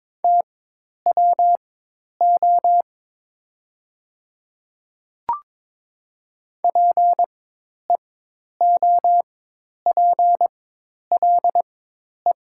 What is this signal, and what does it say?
Morse code, heard as a keyed tone, about 22 wpm, spaced as for 10 wpm, 700 Hz. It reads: TWO PEOPLE
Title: 4x Character Spacing - Top 100 Words - 22wpm